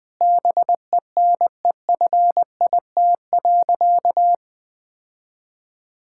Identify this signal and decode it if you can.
Morse code, heard as a keyed tone, about 20 wpm, 700 Hz. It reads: BENEFIT.
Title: Sentences from Top 1000 Words - Review Part 6 - 20wpm